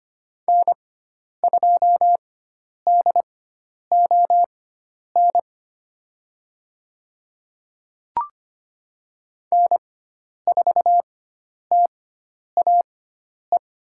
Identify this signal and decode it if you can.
Morse code, heard as a keyed tone, about 25 wpm, spaced as for 10 wpm, 700 Hz. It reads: N2DON N4TAE
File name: Lesson 16g - Letter c - Call Signs 25wpm